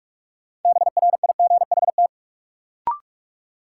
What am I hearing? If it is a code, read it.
Morse code, heard as a keyed tone, about 46 wpm, 700 Hz. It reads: BRIGHT
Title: Sentences from Top 1000 Words - New Word Drill - 45wpm